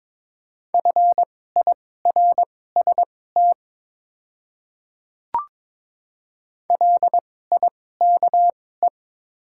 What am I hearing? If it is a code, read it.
Morse code, heard as a keyed tone, about 22 wpm, spaced as for 16 wpm, 700 Hz. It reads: FIRST LIKE
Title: 2x Character Spacing - Top 100 Words - 22wpm